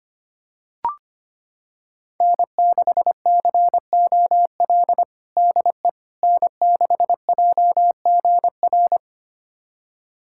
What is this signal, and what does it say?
Morse code, heard as a keyed tone, about 25 wpm, 700 Hz. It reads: N6COL DE N6JGR
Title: USA Callsign DE Callsign - 5 Characters - 25wpm